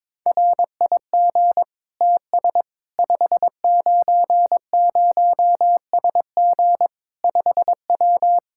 Morse code, heard as a keyed tone, about 22 wpm, 700 Hz. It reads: RIG TS 590SG 5W
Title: Practice QSO - Exchange 2 - Info - 22wpm